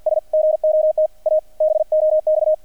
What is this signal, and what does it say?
Morse code, heard as a keyed tone, about 50 wpm, 620 Hz, with radio noise. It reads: S GOT A BOX
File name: archived / CRSnet